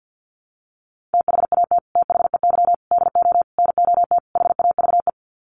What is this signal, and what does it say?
Morse code, heard as a keyed tone, about 50 wpm, 700 Hz. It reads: THAT THEY DO NOT HAVE
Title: Sets of 5 Words Encore2 50wpm